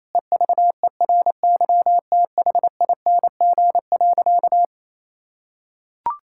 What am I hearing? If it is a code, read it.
Morse code, heard as a keyed tone, about 27 wpm, 700 Hz. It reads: EVERYTHING.
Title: Sentences from Top 1600 Words 28wpm